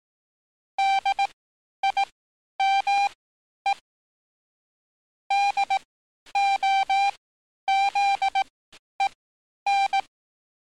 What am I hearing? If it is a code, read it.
Morse code, heard as a keyed tone, about 18 wpm, spaced as for 10 wpm, 790 Hz. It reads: DIME DOZEN
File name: training / NAVPERS 11049to11060 / NAVPERS 11050 RA RB